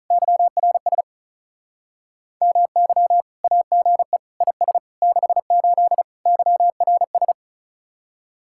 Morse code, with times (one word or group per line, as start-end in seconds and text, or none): 0.1-1.0 YRS
2.4-3.2 MY
3.4-4.2 AGE
4.4-4.8 IS
5.0-6.0 68
6.3-7.3 YRS